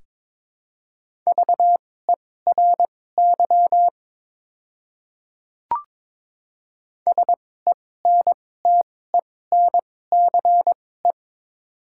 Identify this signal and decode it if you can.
Morse code, heard as a keyed tone, about 22 wpm, spaced as for 16 wpm, 700 Hz. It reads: VERY SENTENCE
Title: 2x Character Spacing - Top 200 Words - 22wpm